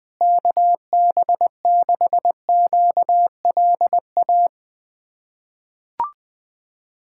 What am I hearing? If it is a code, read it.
Morse code, heard as a keyed tone, about 21 wpm, 700 Hz. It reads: KB6QLA
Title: USA Callsign DE Callsign 20wpm